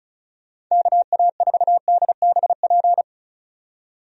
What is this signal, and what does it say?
Morse code, heard as a keyed tone, about 36 wpm, 700 Hz. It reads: KA4DBP
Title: USA Call Signs - 6 Characters - 35wpm